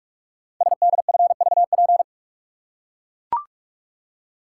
Morse code, heard as a keyed tone, about 44 wpm, 700 Hz. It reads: SDFVP
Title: Code Groups - 5 Letter:Numbers - 45wpm